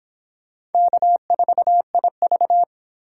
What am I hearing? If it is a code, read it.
Morse code, heard as a keyed tone, about 29 wpm, 700 Hz. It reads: K4IV